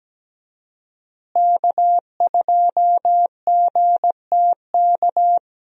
Morse code, heard as a keyed tone, about 17 wpm, 700 Hz. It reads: K2GTK
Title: USA Call Signs - 5 Characters - Repeated 3x - 17wpm